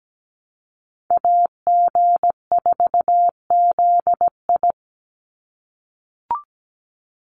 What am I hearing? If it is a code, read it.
Morse code, heard as a keyed tone, about 17 wpm, 700 Hz. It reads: AG4ZI
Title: USA Call Signs 17wpm